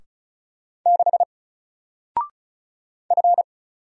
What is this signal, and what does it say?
Morse code, heard as a keyed tone, about 36 wpm, 700 Hz. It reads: 6 F
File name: Single-Letter-Number-35wpm